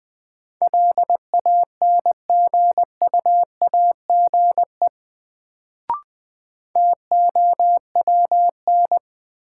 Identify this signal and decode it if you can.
Morse code, heard as a keyed tone, about 20 wpm, 700 Hz. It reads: LANGUAGE TOWN
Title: Sentences from Top 500 Words - New Word Drill - 20wpm